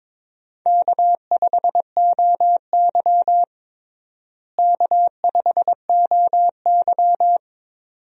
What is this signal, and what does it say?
Morse code, heard as a keyed tone, about 22 wpm, 700 Hz. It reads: K5OY K5OY
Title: USA Call Signs - 4 Characters - Repeated 2x - 22wpm